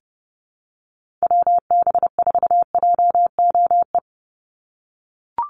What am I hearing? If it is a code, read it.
Morse code, heard as a keyed tone, about 30 wpm, 700 Hz. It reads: WB4JOE
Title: USA Call Signs 30wpm